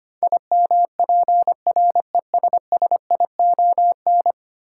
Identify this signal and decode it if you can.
Morse code, heard as a keyed tone, about 25 wpm, 700 Hz. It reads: IMPRESSION
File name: Ten Letter Words - 25wpm